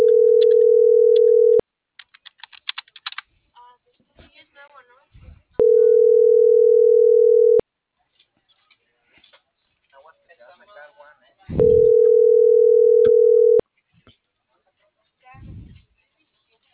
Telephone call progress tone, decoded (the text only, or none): ringback tone